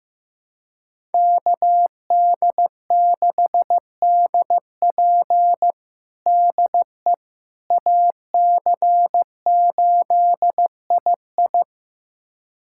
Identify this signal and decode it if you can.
Morse code, heard as a keyed tone, about 15 wpm, 700 Hz. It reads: KD6DP DE AC8II
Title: USA Callsign DE Callsign - 5 Characters - 15wpm